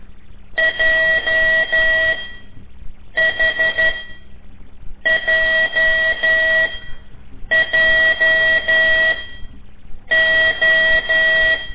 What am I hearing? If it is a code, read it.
Morse code, heard as a keyed tone, about 12 wpm, spaced as for 6 wpm, 1950 Hz. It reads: JHJJO
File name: ik1weg2 / CORSI CW / lez5-1